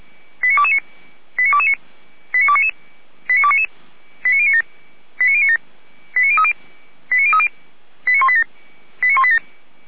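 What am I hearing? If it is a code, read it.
Five-tone selective call (ZVEI), heard as quick runs of five tones, tones of about 80 ms. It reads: 89229 89229 89299 89299 89097 89097 89030 89030 89187 89187
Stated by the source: Five-tone ZVEI selective call received on an SDR.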